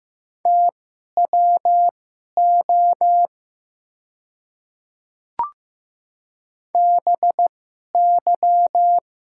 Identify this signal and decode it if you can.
Morse code, heard as a keyed tone, about 15 wpm, spaced as for 11 wpm, 690 Hz. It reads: TWO BY
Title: 2x Character Spacing - Top 100 Words - 15wpm